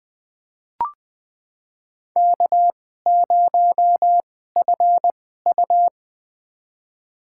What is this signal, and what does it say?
Morse code, heard as a keyed tone, about 20 wpm, spaced as for 14 wpm, 700 Hz. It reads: K0FU
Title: USA Call Signs - 4 Characters - 2x Character Spacing - 20wpm